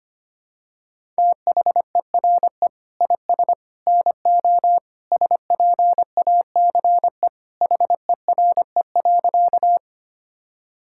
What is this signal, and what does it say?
Morse code, heard as a keyed tone, about 25 wpm, 700 Hz. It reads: THERE IS NO SPACE HERE.
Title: Sentences from Top 1000 Words - Review Part 5 - 25wpm